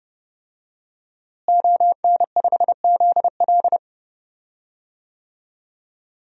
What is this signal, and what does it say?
Morse code, heard as a keyed tone, about 30 wpm, 700 Hz. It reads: ON5ZL
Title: DXCC-100 Call Signs - 5 Characters - Repeated 2x - 30wpm